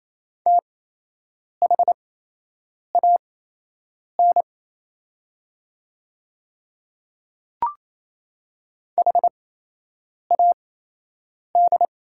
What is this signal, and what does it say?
Morse code, heard as a keyed tone, about 27 wpm, spaced as for 8 wpm, 690 Hz. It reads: THAN HAD
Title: 8x Character Spacing - Top 100 Words - 28wpm